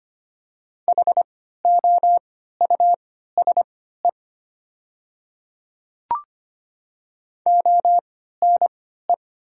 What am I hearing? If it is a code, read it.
Morse code, heard as a keyed tone, about 25 wpm, spaced as for 14 wpm, 700 Hz. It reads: HOUSE ONE
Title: 3x Character Spacing - Top 200 Words - 25wpm